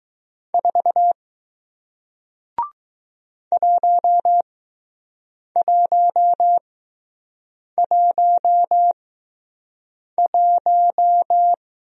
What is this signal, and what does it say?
Morse code, heard as a keyed tone, about 20 wpm, spaced as for 6 wpm, 700 Hz. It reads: 4 1111